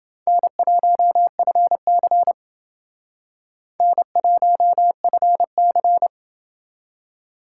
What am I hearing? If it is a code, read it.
Morse code, heard as a keyed tone, about 29 wpm, 700 Hz. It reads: N1FC N1FC